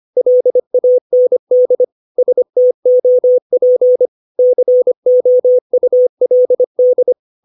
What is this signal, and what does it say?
Morse code, heard as a keyed tone, about 25 wpm, 500 Hz. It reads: LAND STOP COULD
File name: 632 words / 632 Words 25WPM